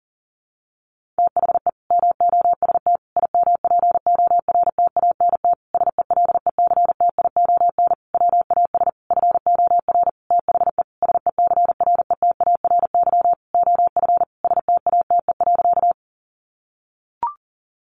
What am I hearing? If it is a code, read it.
Morse code, heard as a keyed tone, about 37 wpm, 700 Hz. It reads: THE MOST IMPORTANT SELECTION WAS FOR THE SECRETARY OF STATE.
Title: Sentences from Top 1400 Words 40wpm